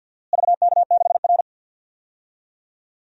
Morse code, heard as a keyed tone, about 46 wpm, 700 Hz. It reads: VK6R
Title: DXCC-100 Call Signs - 4 Characters - Repeated 2x - 50wpm